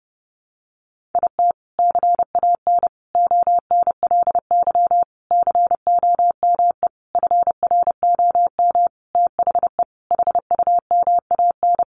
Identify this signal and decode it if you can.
Morse code, heard as a keyed tone, about 30 wpm, 700 Hz. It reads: IT CAN ONLY COME FROM THE HUMAN
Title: Sentences from Top 1400 Words 30wpm